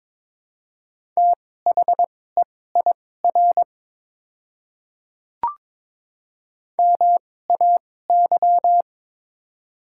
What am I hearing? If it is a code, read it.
Morse code, heard as a keyed tone, about 22 wpm, spaced as for 16 wpm, 700 Hz. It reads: THEIR MAY
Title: 2x Character Spacing - Top 100 Words - 22wpm